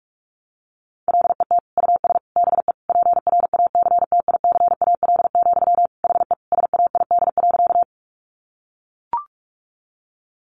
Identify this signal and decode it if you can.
Morse code, heard as a keyed tone, about 44 wpm, 700 Hz. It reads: LET US BE PRACTICAL, HE SAID.
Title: Sentences from Top 1400 Words 45wpm